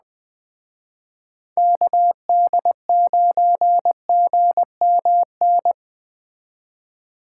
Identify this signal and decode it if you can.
Morse code, heard as a keyed tone, about 20 wpm, 700 Hz. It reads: KD9GMN